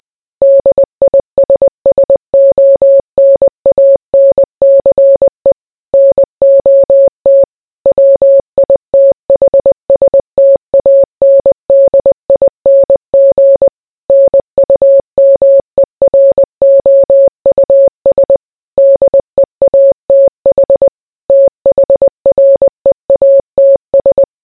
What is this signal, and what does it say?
Morse code, heard as a keyed tone, about 20 wpm, 550 Hz. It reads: DISSONANCE NOT WITHSTANDING NUMEROUS DEATH THREATS